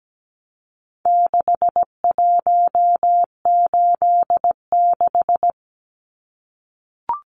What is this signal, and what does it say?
Morse code, heard as a keyed tone, about 17 wpm, 700 Hz. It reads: 6186
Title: Numbers - 4-Digit 17wpm